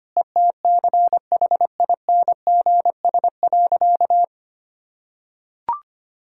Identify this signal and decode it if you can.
Morse code, heard as a keyed tone, about 25 wpm, 700 Hz. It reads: ETCHINGS.
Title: Pangram Challenge v2 - 25wpm